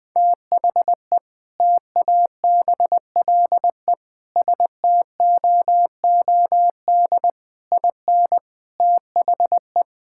Morse code, heard as a keyed tone, about 20 wpm, 700 Hz. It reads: THE TABLE STOOD IN THE
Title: Sentences with 4-Word Phrases - Spoken Beginning - 20wpm